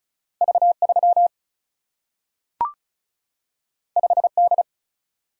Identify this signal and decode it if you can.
Morse code, heard as a keyed tone, about 36 wpm, 700 Hz. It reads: V 3 5 D
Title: Single-Letter Single-Number 35wpm